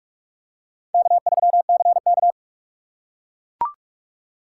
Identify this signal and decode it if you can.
Morse code, heard as a keyed tone, about 38 wpm, 700 Hz. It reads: K3CK